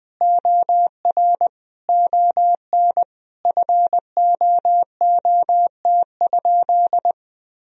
Morse code, heard as a keyed tone, about 20 wpm, 700 Hz. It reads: OR ON FOOT?
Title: Sentences from Top 1000 Words - Review Part 3 - 20wpm